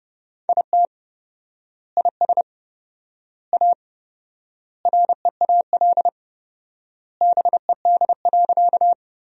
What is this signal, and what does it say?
Morse code, heard as a keyed tone, about 30 wpm, 700 Hz. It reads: IT IS A REAL BED.
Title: 4x Word-Spacing - Sentences from Top 500 Words - 30wpm